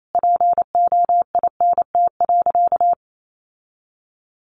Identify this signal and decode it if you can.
Morse code, heard as a keyed tone, about 28 wpm, 700 Hz. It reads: POINT.
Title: Sentences from Top 1300 Words 28wpm